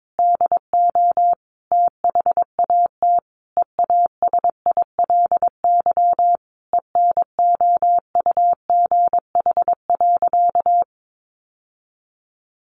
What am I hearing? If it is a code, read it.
Morse code, heard as a keyed tone, about 22 wpm, 700 Hz. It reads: DO THAT EASILY ENOUGH.
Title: Sentences from Top 900 Words 22wpm